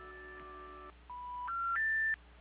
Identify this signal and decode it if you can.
Special information tone, with noise.
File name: recordings / PortlandUnion Millennium